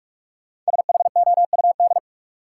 Morse code, heard as a keyed tone, about 46 wpm, 700 Hz. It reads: SHOUD
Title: Sentences from Top 1000 Words - Review Part 4 - New Word Drill - 45wpm